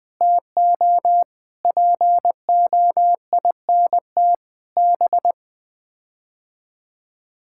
Morse code, heard as a keyed tone, about 20 wpm, 700 Hz. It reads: TO POINT B
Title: Sets of 6 Words - 20wpm